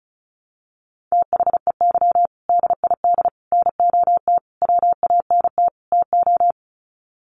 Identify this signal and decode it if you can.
Morse code, heard as a keyed tone, about 34 wpm, 700 Hz. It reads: THEY DID NOT WANT TO
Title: Sets of 5 Words 35wpm